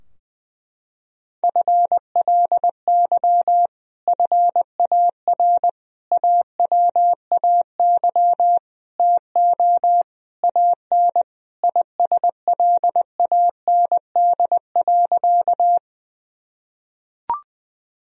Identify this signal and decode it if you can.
Morse code, heard as a keyed tone, about 20 wpm, 700 Hz. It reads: FLY FAR AWAY TO AN ISLAND.
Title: Sentences from Top 1000 Words - Review Part 2 - 20wpm